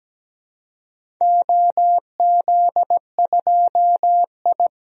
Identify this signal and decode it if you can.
Morse code, heard as a keyed tone, about 17 wpm, 700 Hz. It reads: OZ2I
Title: DXCC-100 Call Signs - 4 Characters - Repeated 3x - 17wpm